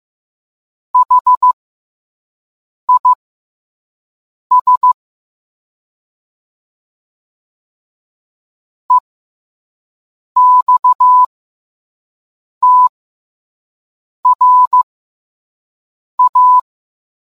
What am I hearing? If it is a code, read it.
Morse code, heard as a keyed tone, about 15 wpm, spaced as for 5 wpm, 1000 Hz. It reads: HIS EXTRA